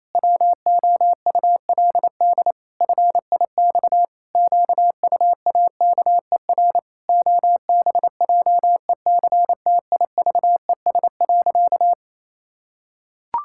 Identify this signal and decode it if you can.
Morse code, heard as a keyed tone, about 28 wpm, 700 Hz. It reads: WOULD FIX QUAKER OBJECTIVES.
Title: Pangram Challenge v2 - 28wpm